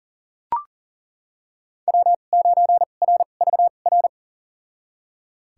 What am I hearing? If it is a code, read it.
Morse code, heard as a keyed tone, about 40 wpm, spaced as for 29 wpm, 700 Hz. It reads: W9RVR
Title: USA Call Signs - 2x Character Spacing - 40wpm